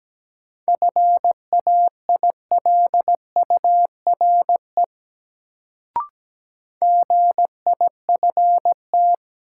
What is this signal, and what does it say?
Morse code, heard as a keyed tone, about 17 wpm, 700 Hz. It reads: FAILURE GIFT
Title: Sentences from Top 900 Words - New Word Drill - 17wpm